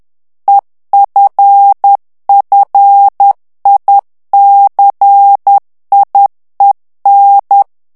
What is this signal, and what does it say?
Morse code, heard as a keyed tone, about 10 wpm, 800 Hz. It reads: EFFICIEN
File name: ANGLAIS texte clair 3 vitesse 10 mots minute